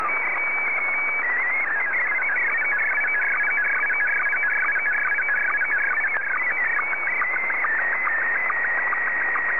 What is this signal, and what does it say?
SSTV (slow-scan television), Scottie 1.